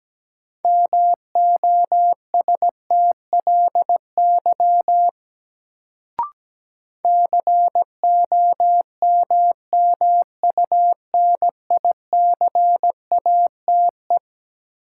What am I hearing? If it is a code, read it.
Morse code, heard as a keyed tone, about 17 wpm, 700 Hz. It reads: MOSTLY COMMUNICATE
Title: Sentences from Top 1300 Words - New Word Drill Alt - 17wpm